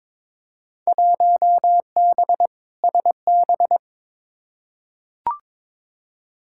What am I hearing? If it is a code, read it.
Morse code, heard as a keyed tone, about 22 wpm, 700 Hz. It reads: 1B SB
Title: Field Day Exchanges - 22wpm